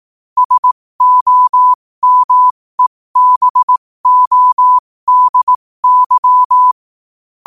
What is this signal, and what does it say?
Morse code, heard as a keyed tone, about 18 wpm, spaced as for 15 wpm, 1000 Hz. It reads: SOMEBODY
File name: qotdfiles / 15wpm